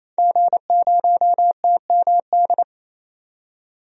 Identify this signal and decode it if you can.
Morse code, heard as a keyed tone, about 28 wpm, 700 Hz. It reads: G0TMD
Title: DXCC-100 Call Signs - 5 Characters - 28wpm